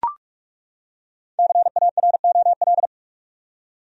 Morse code, heard as a keyed tone, about 41 wpm, 700 Hz. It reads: CAROL
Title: USA Common Names - Max 5 Letters - 45wpm